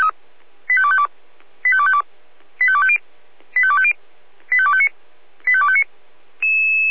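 Five-tone selective call (ZVEI), heard as quick runs of five tones, tones of about 70 ms. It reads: 85252 85252 85250 85250 85259 85259
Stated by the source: ZVEI five-tone selective call received on an SDR.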